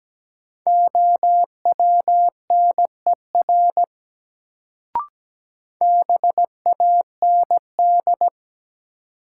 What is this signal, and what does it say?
Morse code, heard as a keyed tone, about 17 wpm, 700 Hz. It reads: OWNER BAND
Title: Sentences from Top 1600 Words - New Word Drill - 17wpm